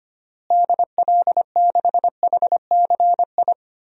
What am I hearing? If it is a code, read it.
Morse code, heard as a keyed tone, about 25 wpm, 700 Hz. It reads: DL6HCI
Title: DXCC-100 Call Signs - 6 Characters - Repeated 3x - 25wpm